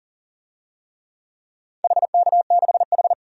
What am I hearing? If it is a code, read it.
Morse code, heard as a keyed tone, about 40 wpm, 700 Hz. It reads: HK6H
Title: DXCC-100 Call Signs - 4 Characters - 40wpm